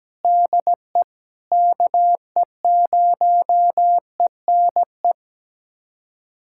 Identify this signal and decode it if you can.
Morse code, heard as a keyed tone, about 17 wpm, 700 Hz. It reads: DE KE0ENE
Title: USA Callsign DE Callsign 17wpm